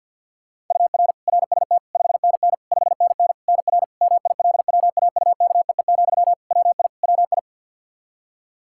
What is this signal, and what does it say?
Morse code, heard as a keyed tone, about 51 wpm, 700 Hz. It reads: UR RST 5NN 5NN NR MILWAUKEE, WI WI
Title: Practice QSO - Exchange 1 - Beginning - 50wpm